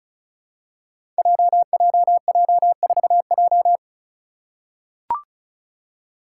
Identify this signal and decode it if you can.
Morse code, heard as a keyed tone, about 36 wpm, 700 Hz. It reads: JJJ4J
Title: Five Challenging Letter-Number Combinations 35wpm